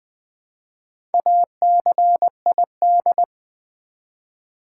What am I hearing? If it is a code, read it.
Morse code, heard as a keyed tone, about 20 wpm, 700 Hz. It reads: ACID
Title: Build A Word - 1 Letter At A Time - 4-Letter Words - 20wpm